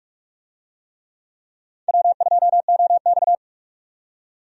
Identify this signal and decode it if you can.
Morse code, heard as a keyed tone, about 44 wpm, 700 Hz. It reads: W2OX
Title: USA Call Signs - 4 Characters - 45wpm